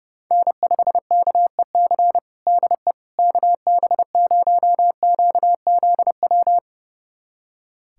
Morse code, heard as a keyed tone, about 30 wpm, 700 Hz. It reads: N5KEC DE KB0QZW
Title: USA Callsign DE Callsign 30wpm